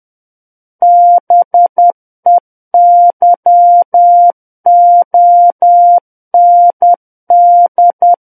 Morse code, heard as a keyed tone, about 10 wpm, 700 Hz. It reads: BEYOND
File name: Book One / 01-Chapter 1 - 10 WPM